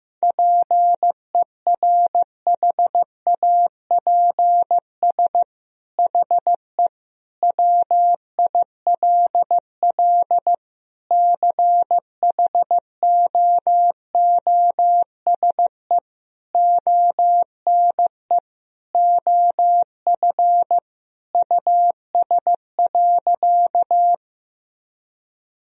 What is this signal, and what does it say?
Morse code, heard as a keyed tone, about 15 wpm, 700 Hz. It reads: PERHAPS HE WILL CHOOSE ONE OF US.
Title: Sentences from Top 1000 Words - Review Part 4 - 15wpm